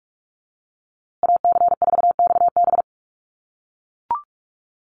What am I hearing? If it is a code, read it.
Morse code, heard as a keyed tone, about 44 wpm, 700 Hz. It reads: AC4XB
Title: USA Call Signs Repeated 2x 45wpm